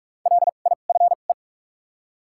Morse code, heard as a keyed tone, about 47 wpm, spaced as for 33 wpm, 700 Hz. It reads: LIFE
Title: 2x Character Spacing - Top 500 Words - 45wpm